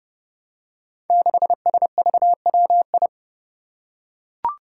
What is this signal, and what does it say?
Morse code, heard as a keyed tone, about 30 wpm, 700 Hz. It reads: 6SVWI
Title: Five Challenging Letter-Number Combinations 30wpm